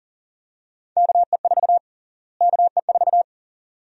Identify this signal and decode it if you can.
Morse code, heard as a keyed tone, about 40 wpm, 700 Hz. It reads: KE4 KE4
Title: USA Call Sign Prefixes - Repeated 2x - 40wpm